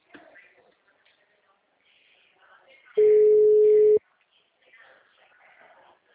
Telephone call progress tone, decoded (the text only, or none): ringback tone